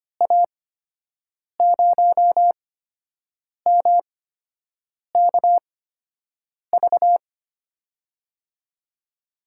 Morse code, heard as a keyed tone, about 25 wpm, spaced as for 7 wpm, 700 Hz. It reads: A0MKV
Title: USA Call Signs - 6 Characters - 8x Character Spacing - 25wpm